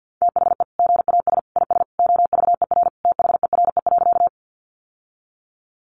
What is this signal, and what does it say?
Morse code, heard as a keyed tone, about 48 wpm, 700 Hz. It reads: THE GAS IS OVER THERE.
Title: Sentences from Top 1100 Words 50wpm